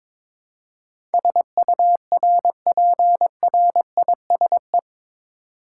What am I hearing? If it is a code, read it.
Morse code, heard as a keyed tone, about 22 wpm, 700 Hz. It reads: SURPRISE
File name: practice / 22wpm / Eight Letter Words - 22wpm